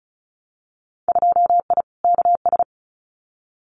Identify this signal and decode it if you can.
Morse code, heard as a keyed tone, about 34 wpm, 700 Hz. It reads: 2I KS
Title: Winter Field Day Exchanges Encore 35wpm